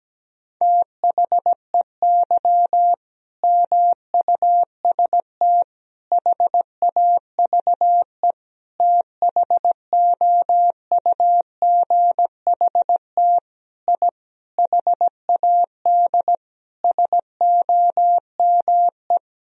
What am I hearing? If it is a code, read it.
Morse code, heard as a keyed tone, about 17 wpm, 700 Hz. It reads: THEY MUST HAVE THOUGHT I HAD SOME
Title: Sentences from Top 1000 Words - Review Part 4 - 17wpm